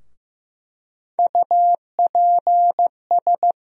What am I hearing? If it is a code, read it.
Morse code, heard as a keyed tone, about 15 wpm, 700 Hz. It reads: UPS
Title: USA Call Sign Postfixes - 3 Letters 15wpm